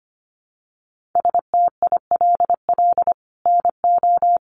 Morse code, heard as a keyed tone, about 25 wpm, 700 Hz. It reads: STILL NO
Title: Sentences from Top 1400 Words 25wpm